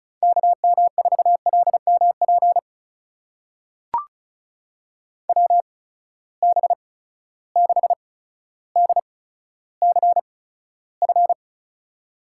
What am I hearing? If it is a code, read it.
Morse code, heard as a keyed tone, about 36 wpm, spaced as for 15 wpm, 700 Hz. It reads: KM4LMP WB6DCF
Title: USA Call Signs - 6 Characters - 8x Character Spacing - 35wpm